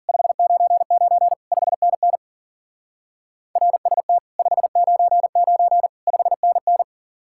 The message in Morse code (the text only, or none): 599 5NN RST 599 5NN